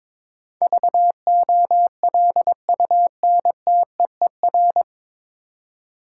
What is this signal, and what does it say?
Morse code, heard as a keyed tone, about 22 wpm, 700 Hz. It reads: VOLUNTEER
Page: Morse Code Ninja practice 22wpm